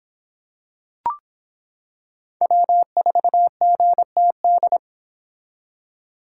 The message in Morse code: W4GTD

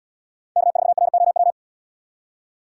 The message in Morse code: K4ROW

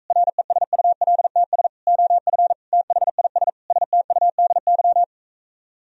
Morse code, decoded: RESULTS OF THIS STUDY

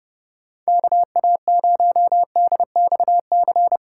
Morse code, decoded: KA0DXC

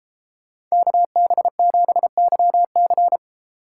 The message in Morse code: KB7YC

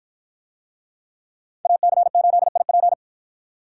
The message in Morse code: AC8IP

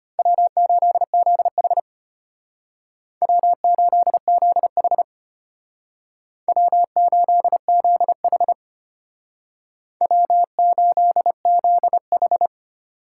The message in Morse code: W8ZH W8ZH W8ZH W8ZH

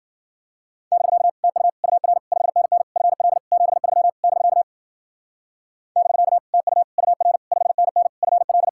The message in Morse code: <BK> TU UR 5NN FL 73 <BK> <BK> TU UR 5NN FL

